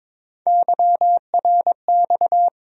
Morse code, heard as a keyed tone, about 21 wpm, 700 Hz. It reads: YRX
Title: USA Call Sign Postfixes - 3 Letters 22wpm